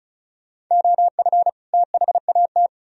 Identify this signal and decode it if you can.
Morse code, heard as a keyed tone, about 46 wpm, 700 Hz. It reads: OF THAT